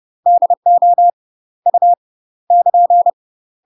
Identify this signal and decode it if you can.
Morse code, heard as a keyed tone, about 30 wpm, 700 Hz. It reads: DO U <KN>
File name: UK MASTER QSOs TRAINING / 30WPM / UK MASTER QSOs 0014